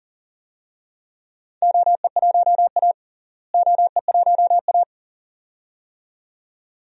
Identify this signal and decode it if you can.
Morse code, heard as a keyed tone, about 40 wpm, 700 Hz. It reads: OE1A OE1A
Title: DXCC-100 Call Signs - 4 Characters - Repeated 2x - 40wpm